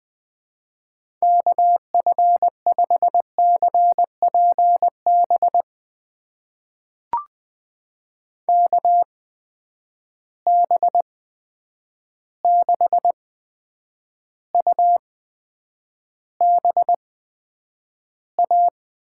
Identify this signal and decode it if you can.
Morse code, heard as a keyed tone, about 20 wpm, spaced as for 9 wpm, 700 Hz. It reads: KF5CPB KB6UBA
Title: USA Call Signs - 6 Characters - 8x Character Spacing - 20wpm